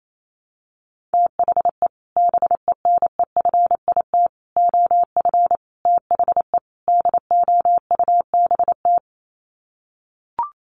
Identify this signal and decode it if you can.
Morse code, heard as a keyed tone, about 28 wpm, 690 Hz. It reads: THE BENEFIT OF THE DOUBT
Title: Sets of 5 Words Encore 28wpm